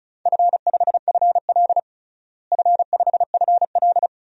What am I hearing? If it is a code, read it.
Morse code, heard as a keyed tone, about 36 wpm, 700 Hz. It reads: F5FL F5FL
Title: DXCC-100 Call Signs - 4 Characters - Repeated 2x - 35wpm